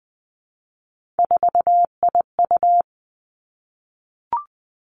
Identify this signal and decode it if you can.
Morse code, heard as a keyed tone, about 20 wpm, 700 Hz. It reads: 4IU
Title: Three Challenging Letter-Number Combinations 20wpm